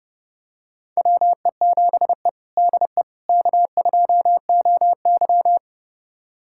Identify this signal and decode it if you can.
Morse code, heard as a keyed tone, about 30 wpm, 700 Hz. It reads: WE7E DE K2OY
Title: USA Callsign DE Callsign - 4 Characters - 30wpm